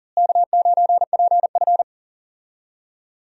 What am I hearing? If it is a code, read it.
Morse code, heard as a keyed tone, about 38 wpm, 700 Hz. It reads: K9PF
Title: USA Call Signs - 4 Characters - Repeated 3x - 40wpm